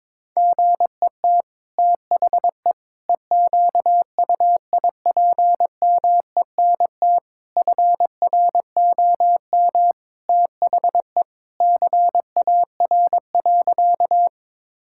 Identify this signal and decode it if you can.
Morse code, heard as a keyed tone, about 21 wpm, 700 Hz. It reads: GET THE EQUIPMENT FROM THE CAR.
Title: Sentences from Top 1000 Words - Review Part 5 - 22wpm